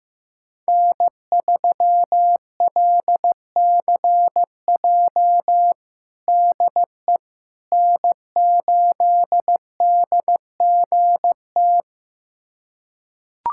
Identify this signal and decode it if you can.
Morse code, heard as a keyed tone, about 15 wpm, 700 Hz. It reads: N3LCJ DE N8DGT
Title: USA Callsign DE Callsign 15wpm